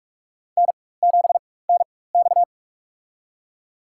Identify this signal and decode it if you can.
Morse code, heard as a keyed tone, about 41 wpm, spaced as for 20 wpm, 690 Hz. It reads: N7NX
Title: USA Call Signs - 4x Character Spacing - 45wpm